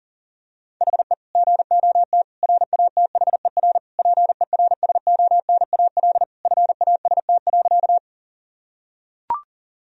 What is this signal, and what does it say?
Morse code, heard as a keyed tone, about 40 wpm, 700 Hz. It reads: HE GOT RATHER PERSONAL FAST.
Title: Sentences from Top 1000 Words - Review Part 4 - 40wpm